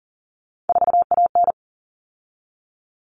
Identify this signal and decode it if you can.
Morse code, heard as a keyed tone, about 40 wpm, 700 Hz. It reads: 4AN
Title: All Cut Numbers - 3-Digit 40wpm